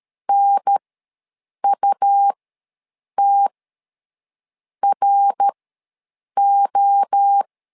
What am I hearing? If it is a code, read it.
Morse code, heard as a keyed tone, about 13 wpm, spaced as for 6 wpm, 800 Hz. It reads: NUTRO
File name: LEZIONI / 3 MWRO / 3 MWRO ws 12wpm 1914